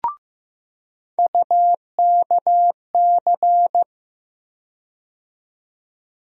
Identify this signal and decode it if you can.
Morse code, heard as a keyed tone, about 15 wpm, 700 Hz. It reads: UKC
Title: USA Call Sign Postfixes - 3 Letters 15wpm